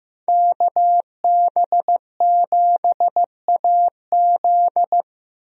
Morse code, heard as a keyed tone, about 15 wpm, 700 Hz. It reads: KB7AZ
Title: USA Call Signs - 5 Characters - Repeated 3x - 15wpm